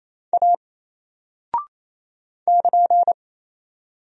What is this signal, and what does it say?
Morse code, heard as a keyed tone, about 28 wpm, 700 Hz. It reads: A <KN>
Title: Single Character (Letter:Number:Punctuation:Prosign) 28wpm